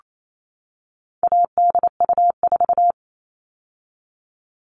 Morse code, heard as a keyed tone, about 28 wpm, 700 Hz. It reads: ADU4
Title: All Cut Numbers - 4-Digit 28wpm